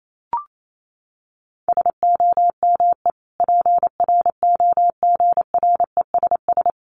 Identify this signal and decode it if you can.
Morse code, heard as a keyed tone, about 28 wpm, 700 Hz. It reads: SOME PROGRESS
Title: Sentences from Top 1400 Words 28wpm